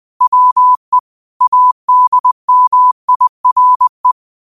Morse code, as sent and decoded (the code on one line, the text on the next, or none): .-- . / .- -.. -- .. .-. .
WE ADMIRE